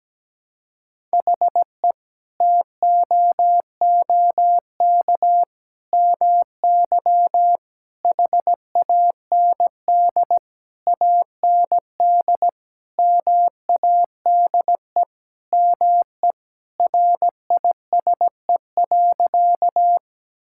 Morse code, heard as a keyed tone, about 17 wpm, 700 Hz. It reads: HE TOOK MY HAND AND MADE ME RISE.
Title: Sentences from Top 1000 Words - Review Part 6 - 17wpm